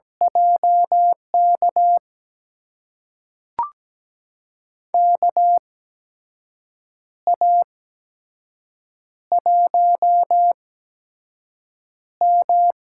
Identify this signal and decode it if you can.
Morse code, heard as a keyed tone, about 17 wpm, spaced as for 6 wpm, 700 Hz. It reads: JK KA1M
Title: USA Call Signs - 6 Characters - 8x Character Spacing - 17wpm